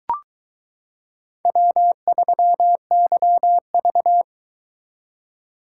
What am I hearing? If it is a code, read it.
Morse code, heard as a keyed tone, about 22 wpm, 700 Hz. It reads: W3YV